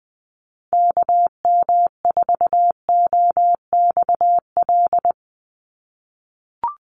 Morse code, heard as a keyed tone, about 20 wpm, 700 Hz. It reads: KM4OXL
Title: USA Call Signs 20wpm